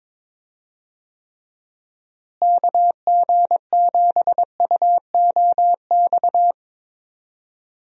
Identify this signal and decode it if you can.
Morse code, heard as a keyed tone, about 22 wpm, 700 Hz. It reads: KG7UOX
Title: USA Call Signs - 6 Characters - Repeated 2x - 22wpm